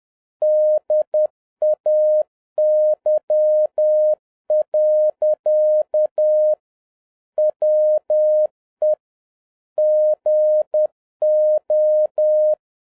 Morse code, heard as a keyed tone, about 10 wpm, 610 Hz. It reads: DAY. WE GO
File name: 2011-07-mp3 / n31 THE 3.11 2 10wpm